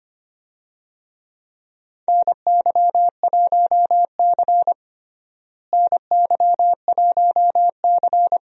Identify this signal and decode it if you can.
Morse code, heard as a keyed tone, about 25 wpm, 700 Hz. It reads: NY1C NY1C
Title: USA Call Signs - 4 Characters - Repeated 2x - 25wpm